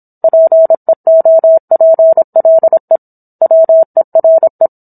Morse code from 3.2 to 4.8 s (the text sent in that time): WERE